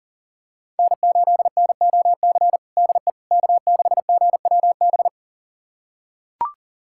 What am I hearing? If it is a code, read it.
Morse code, heard as a keyed tone, about 40 wpm, 700 Hz. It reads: N8NOC DE K6GWB
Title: USA Callsign DE Callsign - 5 Characters - 40wpm